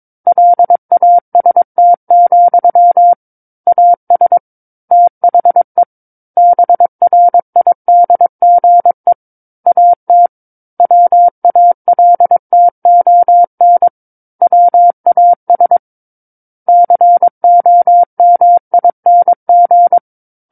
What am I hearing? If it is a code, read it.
Morse code, heard as a keyed tone, about 22 wpm, 700 Hz. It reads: LAST, AS THE BRIDGE AT WALTON WAS COMING
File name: Book One / 13-Chapter 13 - 22 WPM